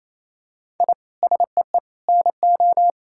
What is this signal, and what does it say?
Morse code, heard as a keyed tone, about 28 wpm, 700 Hz. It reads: I SEE NO
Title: Sentences from Top 1000 Words - Review Part 5 - 28wpm